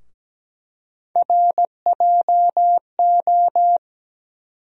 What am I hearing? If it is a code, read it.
Morse code, heard as a keyed tone, about 17 wpm, 700 Hz. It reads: RJO
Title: USA Call Sign Postfixes - 3 Letters 17wpm